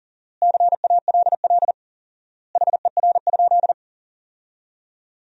Morse code, heard as a keyed tone, about 40 wpm, 700 Hz. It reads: CALL HER?
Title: 4x Word-Spacing - Sentences from Top 100 Words - 40wpm